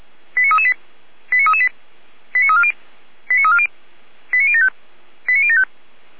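Five-tone selective call (ZVEI), heard as quick runs of five tones, tones of about 70 ms. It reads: 89338 89338 89340 89340 89075 89075
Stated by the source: Five-tone ZVEI selective call received on an SDR.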